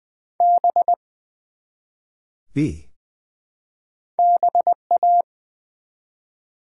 Morse code, heard as a keyed tone, about 20 wpm, 700 Hz. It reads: B BA
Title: Build A Word - 1 Letter At A Time - 5-Letter Words - 20wpm